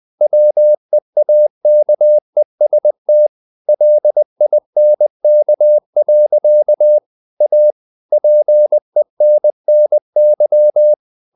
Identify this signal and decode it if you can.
Morse code, heard as a keyed tone, about 20 wpm, 600 Hz. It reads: WEAKEST LINK. A PENNY
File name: CW / Practice / EnglishSayings20